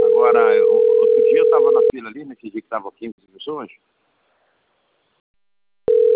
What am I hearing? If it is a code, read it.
Ringback tone (North America), 440 + 480 Hz, with speech.
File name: rg-1600-unknown- - - .98917